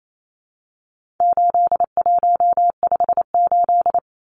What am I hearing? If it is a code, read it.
Morse code, heard as a keyed tone, about 28 wpm, 700 Hz. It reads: 8158
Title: Numbers - 4-Digit 28wpm